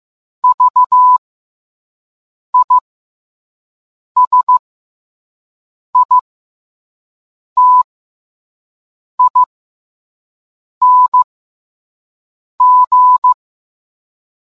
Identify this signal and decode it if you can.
Morse code, heard as a keyed tone, about 15 wpm, spaced as for 5 wpm, 1000 Hz. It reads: VISITING